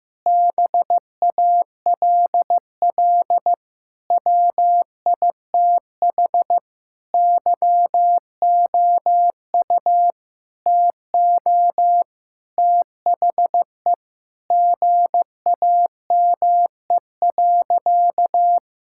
Morse code, heard as a keyed tone, about 15 wpm, 700 Hz. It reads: BALL WITH YOU TO THE GAME.
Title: Sentences from Top 1000 Words - Review Part 6 - 15wpm